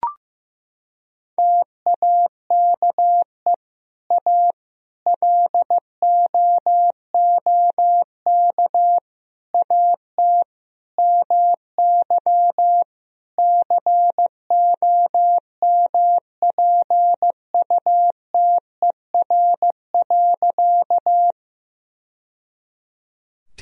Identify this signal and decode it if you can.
Morse code, heard as a keyed tone, about 15 wpm, 700 Hz. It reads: TAKE A LOOK AT MY COMPUTER.
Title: Sentences from Top 1000 Words - Review Part 3 - 15wpm